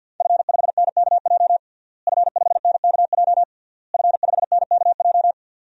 Morse code, heard as a keyed tone, about 50 wpm, 700 Hz. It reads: F5NKJ F5NKJ F5NKJ
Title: DXCC-100 Call Signs - 5 Characters - Repeated 3x - 50wpm